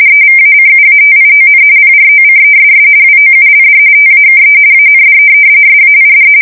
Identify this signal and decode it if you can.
RTTY, about 45.45 baud. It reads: EVACUATION PROCESS WOULD TAKE AND SAID
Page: 170 Shift 2125/2295